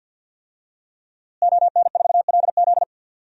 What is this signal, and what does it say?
Morse code, heard as a keyed tone, about 47 wpm, 700 Hz. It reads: ON4LZ
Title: DXCC-100 Call Signs - 5 Characters - Repeated 3x - 50wpm